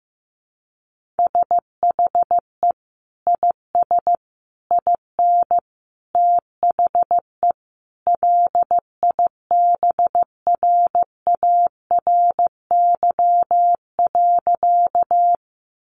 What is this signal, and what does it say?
Morse code, heard as a keyed tone, about 15 wpm, 700 Hz. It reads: SHE IS IN THE LIBRARY.
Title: Sentences from Top 600 Words 15wpm